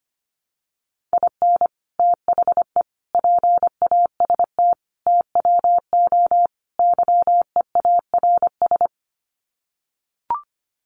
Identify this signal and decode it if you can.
Morse code, heard as a keyed tone, about 25 wpm, 700 Hz. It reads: IN THE PAST TWO YEARS
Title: Sets of 5 Words 25wpm